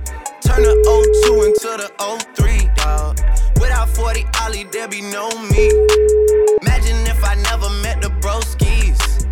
Ringback tone (Europe, 425 Hz), with noise.